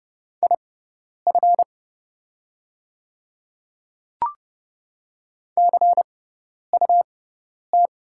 Morse code, heard as a keyed tone, about 30 wpm, spaced as for 10 wpm, 700 Hz. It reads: IF CUT
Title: 6x Character Spacing - Top 300 Words - 30wpm